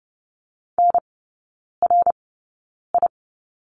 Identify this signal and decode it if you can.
Morse code, heard as a keyed tone, about 30 wpm, spaced as for 9 wpm, 700 Hz. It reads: NRI
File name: Lesson 18e - Letter m - Three Characters 30wpm